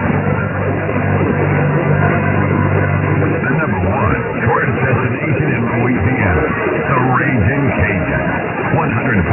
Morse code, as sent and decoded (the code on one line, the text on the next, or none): none